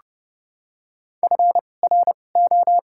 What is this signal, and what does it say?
Morse code, heard as a keyed tone, about 30 wpm, spaced as for 22 wpm, 700 Hz. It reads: FRO
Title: 2x Character Spacing - Top 100 Words - 30wpm